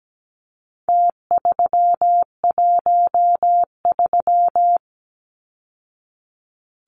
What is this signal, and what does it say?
Morse code, heard as a keyed tone, about 17 wpm, 700 Hz. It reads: T313
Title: Numbers - Contest Style - 4-Digit 17wpm